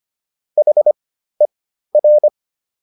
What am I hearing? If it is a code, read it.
Morse code, heard as a keyed tone, about 25 wpm, spaced as for 13 wpm, 600 Hz. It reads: HER